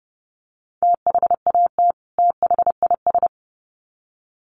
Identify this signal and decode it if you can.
Morse code, heard as a keyed tone, about 30 wpm, 700 Hz. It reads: THAT THIS
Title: Sets of 2 Words 30wpm